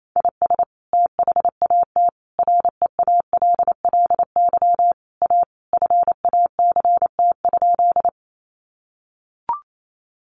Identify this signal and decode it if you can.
Morse code, heard as a keyed tone, about 28 wpm, 700 Hz. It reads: IS THAT REALLY A FACT?
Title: Sentences from Top 600 Words 28wpm